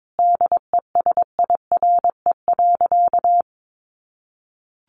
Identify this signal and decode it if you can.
Morse code, heard as a keyed tone, about 22 wpm, 700 Hz. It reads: DESIRE.
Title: Sentences from Top 1100 Words 22wpm